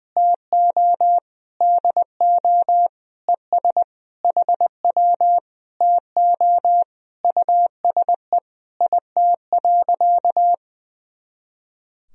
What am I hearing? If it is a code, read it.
Morse code, heard as a keyed tone, about 21 wpm, 700 Hz. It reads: TO DO ES HW TO USE IT.
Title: Sentences from Top 500 Words and Abbreviations 20wpm